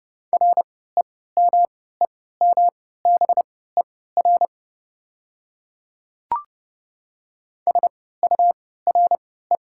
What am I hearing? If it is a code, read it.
Morse code, heard as a keyed tone, about 32 wpm, spaced as for 17 wpm, 700 Hz. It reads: REMEMBER SURE
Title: 3x Character Spacing - Top 500 Words - 30wpm